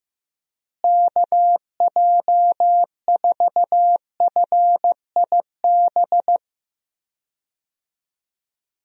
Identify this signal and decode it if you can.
Morse code, heard as a keyed tone, about 15 wpm, 700 Hz. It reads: KJ4FIB
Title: USA Call Signs - 6 Characters - Repeated 3x - 15wpm